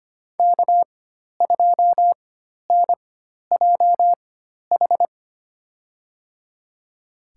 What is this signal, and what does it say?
Morse code, heard as a keyed tone, about 25 wpm, spaced as for 12 wpm, 700 Hz. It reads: K2NJH
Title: USA Call Signs - 5 Characters - 4x Character Spacing - 25wpm